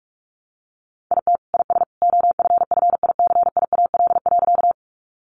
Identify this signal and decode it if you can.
Morse code, heard as a keyed tone, about 44 wpm, 700 Hz. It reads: IT IS OFFICIAL.
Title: Sentences from Top 1000 Words 45wpm